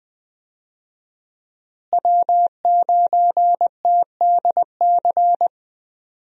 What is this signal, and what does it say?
Morse code, heard as a keyed tone, about 20 wpm, 700 Hz. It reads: W9TDC
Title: USA Call Signs - 5 Characters - Repeated 2x - 20wpm